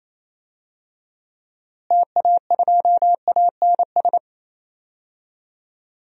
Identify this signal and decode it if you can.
Morse code, heard as a keyed tone, about 27 wpm, 700 Hz. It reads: TA2ANS
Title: DXCC-100 Call Signs - 6 Characters - Repeated 3x - 28wpm